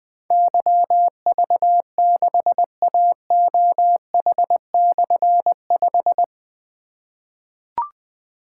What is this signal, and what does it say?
Morse code, heard as a keyed tone, about 20 wpm, 700 Hz. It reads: YV6AOH/5
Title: Difficult Call Signs 20wpm